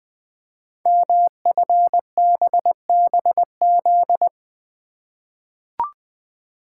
Morse code, heard as a keyed tone, about 20 wpm, 700 Hz. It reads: MFBBZ
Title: Code Groups - 5 Letters - 20wpm